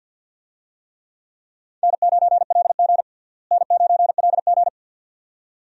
Morse code, heard as a keyed tone, about 52 wpm, 700 Hz. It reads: N9LG N9LG
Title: USA Call Signs - 4 Characters - Repeated 2x - 50wpm